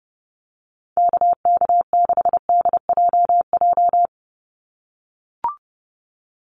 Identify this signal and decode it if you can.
Morse code, heard as a keyed tone, about 30 wpm, 700 Hz. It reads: KK6DJJ
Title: USA Call Signs 30wpm